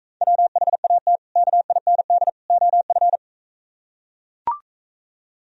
Morse code, heard as a keyed tone, about 42 wpm, 700 Hz. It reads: WHAT KIND OF